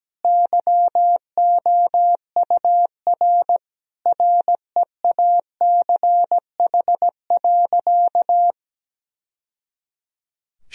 Morse code, heard as a keyed tone, about 17 wpm, 700 Hz. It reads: YOUR REACH.
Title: Sentences from Top 1000 Words - Review Part 2 - 17wpm